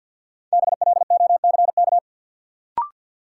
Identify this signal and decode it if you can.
Morse code, heard as a keyed tone, about 53 wpm, 700 Hz. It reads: BLOCK
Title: Sentences from Top 1600 Words - New Word Drill Alt - 50wpm